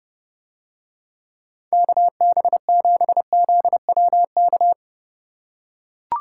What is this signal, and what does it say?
Morse code, heard as a keyed tone, about 30 wpm, 700 Hz. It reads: KB7ZWK